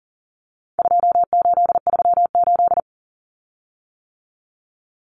Morse code, heard as a keyed tone, about 40 wpm, 700 Hz. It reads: 2838
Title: Common Cut Numbers - 4-Digit 40wpm